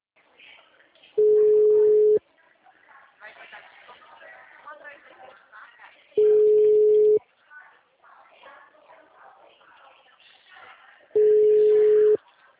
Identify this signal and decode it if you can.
Ringback tone (Europe, 425 Hz), with speech.
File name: out- -2012- - - .39972